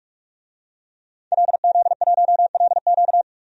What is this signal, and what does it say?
Morse code, heard as a keyed tone, about 44 wpm, 700 Hz. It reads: LZ1LQ
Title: DXCC-100 Call Signs - 5 Characters - 45wpm